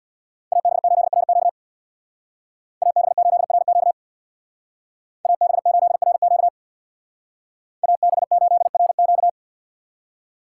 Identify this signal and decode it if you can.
Morse code, heard as a keyed tone, about 60 wpm, 700 Hz. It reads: AB8RQ AB8RQ AB8RQ AB8RQ
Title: Speed Racing - USA Call Signs 50wpm